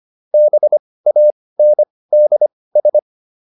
Morse code, heard as a keyed tone, about 25 wpm, spaced as for 18 wpm, 600 Hz. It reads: BANDS
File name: SS211 18